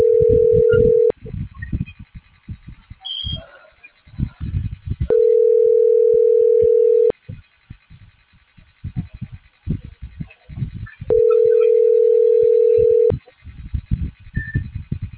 Ringback tone (North America), 440 + 480 Hz.